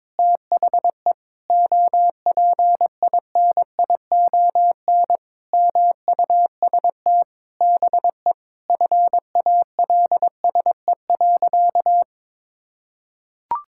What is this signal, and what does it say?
Morse code, heard as a keyed tone, about 22 wpm, 700 Hz. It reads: THE OPINION MUST BE FALSE.
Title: Sentences from Top 1600 Words 22wpm